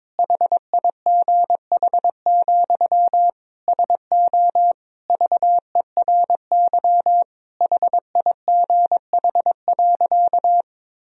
Morse code, heard as a keyed tone, about 22 wpm, 700 Hz. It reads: HIGH, SO VERY HIGH.
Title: Sentences from Top 1000 Words - Review Part 1 - 22wpm